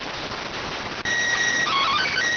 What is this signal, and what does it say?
SSTV (slow-scan television), PD 290.